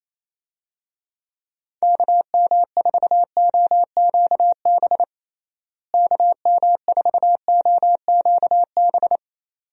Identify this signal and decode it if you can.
Morse code, heard as a keyed tone, about 28 wpm, 700 Hz. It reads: KM4OQB KM4OQB
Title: USA Call Signs - 6 Characters - Repeated 2x - 28wpm